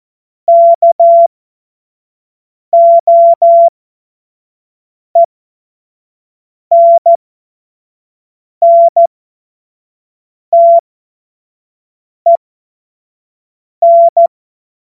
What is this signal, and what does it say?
Morse code, heard as a keyed tone, about 14 wpm, spaced as for 5 wpm, 680 Hz. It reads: KOENNTEN
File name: Morsekurs / Witze / Witz 20 Februar 2024 25 F70